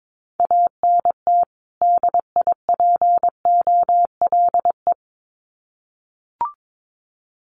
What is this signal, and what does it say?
Morse code, heard as a keyed tone, about 22 wpm, 700 Hz. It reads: ANT DIPOLE
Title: Common QSO Phrases 22wpm